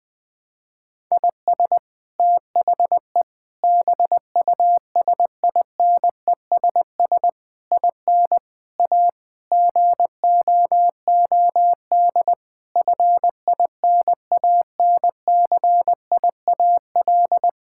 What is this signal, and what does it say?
Morse code, heard as a keyed tone, about 20 wpm, 700 Hz. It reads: IS THE BUSINESS IN A GOOD FINANCIAL
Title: Sentences from Top 1000 Words - Review Part 5 - 20wpm